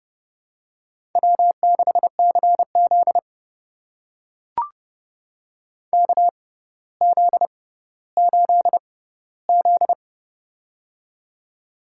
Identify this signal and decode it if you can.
Morse code, heard as a keyed tone, about 29 wpm, spaced as for 15 wpm, 700 Hz. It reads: W6CZ KZ8Z
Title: USA Call Signs - 6x Character Spacing - 30wpm